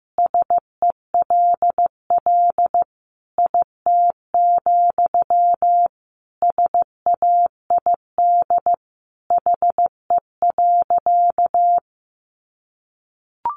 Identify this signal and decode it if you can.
Morse code, heard as a keyed tone, about 15 wpm, 700 Hz. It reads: SELL IT, SAID HE.
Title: Sentences from Top 1100 Words 15wpm